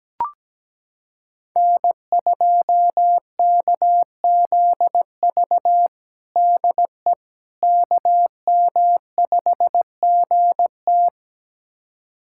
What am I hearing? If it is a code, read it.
Morse code, heard as a keyed tone, about 17 wpm, 700 Hz. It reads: N2KZV DE KM5GT
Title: USA Callsign DE Callsign - 5 Characters - 17wpm